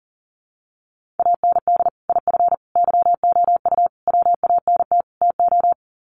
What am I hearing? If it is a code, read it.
Morse code, heard as a keyed tone, about 38 wpm, 700 Hz. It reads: AND IF YOU WANT TO
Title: Sets of 5 Words Encore 40wpm